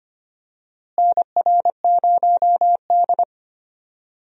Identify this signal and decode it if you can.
Morse code, heard as a keyed tone, about 25 wpm, 700 Hz. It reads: NR0D
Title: USA Call Signs - 4 Characters - Repeated 3x - 25wpm